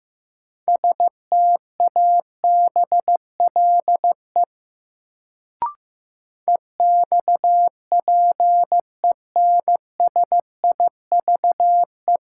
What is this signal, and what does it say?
Morse code, heard as a keyed tone, about 15 wpm, 700 Hz. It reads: STABLE EXPENSIVE
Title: Sentences from Top 1500 Words - New Word Drill Alt - 15wpm